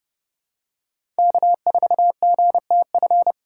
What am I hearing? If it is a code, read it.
Morse code, heard as a keyed tone, about 29 wpm, 700 Hz. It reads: K4GTF
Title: USA Call Signs - 5 Characters - 3x Character Spacing - 30wpm